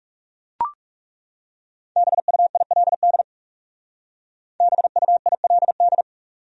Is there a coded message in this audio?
Yes: Morse code from 2.0 to 6.0 s, heard as a keyed tone, about 40 wpm, 700 Hz. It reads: BUILD BUILD